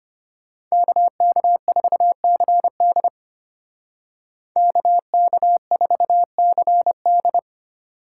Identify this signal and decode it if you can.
Morse code, heard as a keyed tone, about 33 wpm, 700 Hz. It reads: KK4CD KK4CD